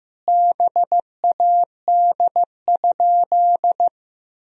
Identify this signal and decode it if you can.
Morse code, heard as a keyed tone, about 16 wpm, 700 Hz. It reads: BAD?
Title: Sentences from Top 1000 Words - Review Part 1 - 15wpm